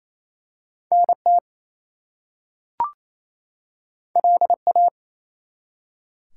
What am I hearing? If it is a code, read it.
Morse code, heard as a keyed tone, about 29 wpm, 700 Hz. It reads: NT LA
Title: US States and Canadian Provinces 28wpm